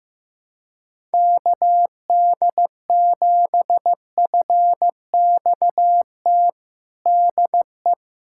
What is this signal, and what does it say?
Morse code, heard as a keyed tone, about 15 wpm, 700 Hz. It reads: KD7FXT DE
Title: USA Callsign DE Callsign - 6 Characters - 15wpm